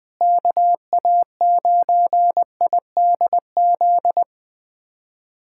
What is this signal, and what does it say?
Morse code, heard as a keyed tone, about 20 wpm, 700 Hz. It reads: KA9IDZ
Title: USA Callsign DE Callsign - 6 Characters - 20wpm